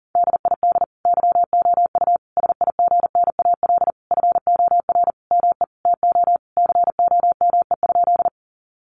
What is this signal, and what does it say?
Morse code, heard as a keyed tone, about 40 wpm, 700 Hz. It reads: DID YOU SIGNAL FOR ME TO COME?
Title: Sentences from Top 1400 Words 40wpm